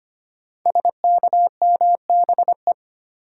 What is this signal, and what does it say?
Morse code, heard as a keyed tone, about 25 wpm, 690 Hz. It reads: SKMBE
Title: Code Groups - 5 Letters - 25wpm